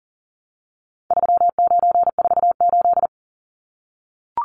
Morse code, heard as a keyed tone, about 40 wpm, 700 Hz. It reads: 3948
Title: Numbers - 4-Digit 40wpm